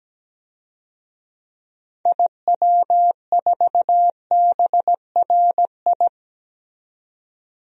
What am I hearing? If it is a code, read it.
Morse code, heard as a keyed tone, about 17 wpm, 700 Hz. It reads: IW4BRI
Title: DXCC-100 Call Signs - 6 Characters - 17wpm